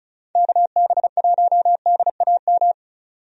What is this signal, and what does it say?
Morse code, heard as a keyed tone, about 34 wpm, 700 Hz. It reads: KB1DAM
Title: USA Call Signs - 6 Characters - 35wpm